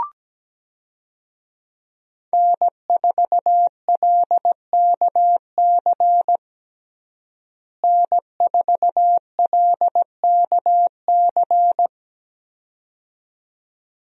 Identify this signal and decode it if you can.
Morse code, heard as a keyed tone, about 17 wpm, 700 Hz. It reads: N4LKC N4LKC
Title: USA Call Signs - 5 Characters - Repeated 2x - 17wpm